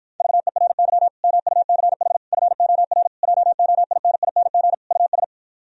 Morse code, heard as a keyed tone, about 52 wpm, 700 Hz. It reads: VERY MUCH FOR JOINING US